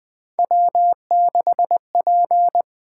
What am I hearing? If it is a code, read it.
Morse code, heard as a keyed tone, about 20 wpm, 700 Hz. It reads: W6P
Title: Three Letter-Number Combinations - 20wpm